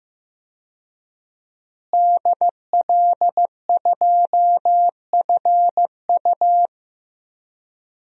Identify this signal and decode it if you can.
Morse code, heard as a keyed tone, about 15 wpm, 700 Hz. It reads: DL2FU
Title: DXCC-100 Call Signs - 5 Characters - Repeated 2x - 15wpm